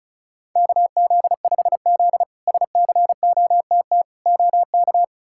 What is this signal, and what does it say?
Morse code, heard as a keyed tone, about 36 wpm, 700 Hz. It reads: KZ5Z SCOTT OK
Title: K1USN SST QSO Practice - 35wpm